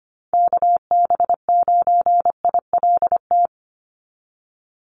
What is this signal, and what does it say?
Morse code, heard as a keyed tone, about 25 wpm, 690 Hz. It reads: KB9ILT
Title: NA SOTA Activator Callsigns 25wpm